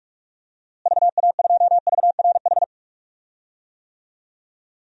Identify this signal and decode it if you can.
Morse code, heard as a keyed tone, about 46 wpm, 700 Hz. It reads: VA2VRH
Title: DXCC-100 Call Signs - 6 Characters - Repeated 3x - 45wpm